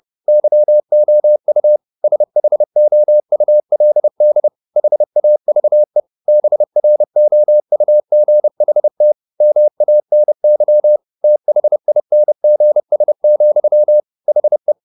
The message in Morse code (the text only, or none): YOU SHOULD HAVE BROUGHT MANY THINGS, HE